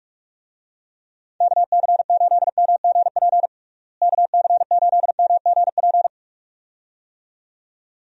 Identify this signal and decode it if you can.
Morse code, heard as a keyed tone, about 43 wpm, 700 Hz. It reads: KC8MGP KC8MGP
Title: USA Call Signs - 6 Characters - Repeated 2x - 45wpm